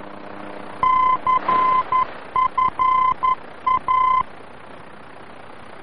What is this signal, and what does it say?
Morse code, heard as a keyed tone, about 11 wpm, 1030 Hz. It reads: CFA
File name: france / mx / CFA-VOR-F2